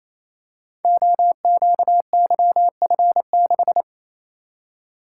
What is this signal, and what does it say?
Morse code, heard as a keyed tone, about 28 wpm, 700 Hz. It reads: OQYF6
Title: Code Groups - 5 Letter:Numbers - 28wpm